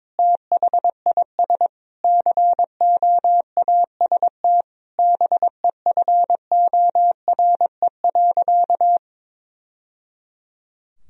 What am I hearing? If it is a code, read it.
Morse code, heard as a keyed tone, about 22 wpm, 700 Hz. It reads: THIS COAST BEFORE.
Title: Sentences from Top 1000 Words - Review Part 4 - 22wpm